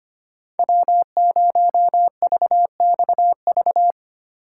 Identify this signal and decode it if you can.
Morse code, heard as a keyed tone, about 25 wpm, 700 Hz. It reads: W0VXV
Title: USA Call Signs - 4x Character Spacing - 25wpm